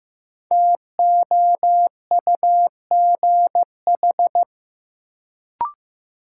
Morse code, heard as a keyed tone, about 16 wpm, 700 Hz. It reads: TOUGH
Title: Sentences from Top 1200 Words - New Word Drill Alt - 15wpm